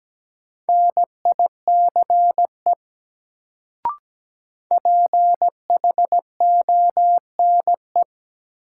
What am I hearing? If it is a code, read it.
Morse code, heard as a keyed tone, about 17 wpm, 700 Hz. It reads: NICE PHONE
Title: Sentences from Top 1100 Words - New Word Drill Alt - 17wpm